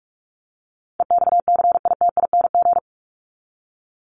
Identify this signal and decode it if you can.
Morse code, heard as a keyed tone, about 44 wpm, 700 Hz. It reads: EXCITING
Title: Top 1751-2000 Words 45wpm